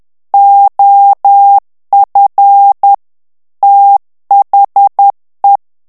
Morse code, heard as a keyed tone, about 10 wpm, 800 Hz. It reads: OF THE
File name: ANGLAIS texte clair 3 vitesse 10 mots minute